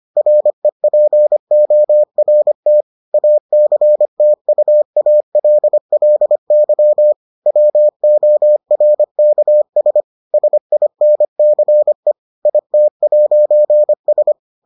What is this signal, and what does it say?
Morse code, heard as a keyed tone, about 25 wpm, 600 Hz. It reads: REPORT ACTUALLY WORKS SINCE IT'S